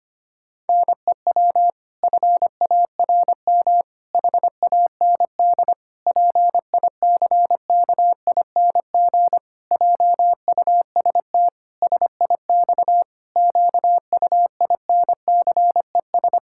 Morse code, heard as a keyed tone, about 25 wpm, 700 Hz. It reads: NEW FARM HAND PICKING JUST SIX QUINCES
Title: Pangram Challenge v2 - 25wpm